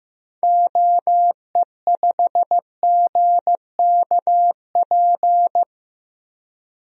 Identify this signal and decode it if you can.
Morse code, heard as a keyed tone, about 15 wpm, 700 Hz. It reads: OE5GKP
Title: DXCC-100 Call Signs - 6 Characters - 15wpm